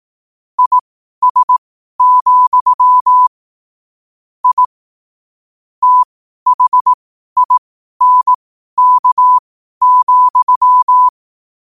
Morse code, heard as a keyed tone, about 18 wpm, spaced as for 12 wpm, 1000 Hz. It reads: IS, I THINK,
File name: qotdfiles / 12wpm